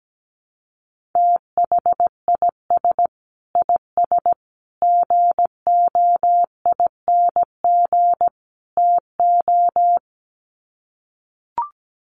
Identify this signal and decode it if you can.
Morse code, heard as a keyed tone, about 17 wpm, 700 Hz. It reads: THIS IS GOING TO
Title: Sets of 4 Words 17wpm